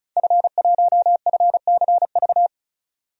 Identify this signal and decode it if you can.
Morse code, heard as a keyed tone, about 34 wpm, 700 Hz. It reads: F1FCV
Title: DXCC-100 Call Signs - 5 Characters - 35wpm